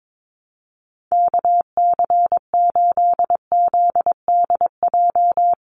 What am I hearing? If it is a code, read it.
Morse code, heard as a keyed tone, about 22 wpm, 700 Hz. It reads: KC8ZDJ
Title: USA Call Signs 22wpm